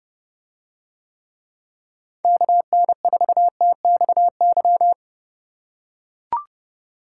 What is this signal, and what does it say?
Morse code, heard as a keyed tone, about 29 wpm, 700 Hz. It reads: KN4TXY